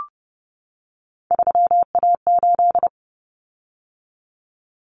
Morse code, heard as a keyed tone, about 30 wpm, 700 Hz. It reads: 3A8
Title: Common Cut Numbers - 3-Digit 30wpm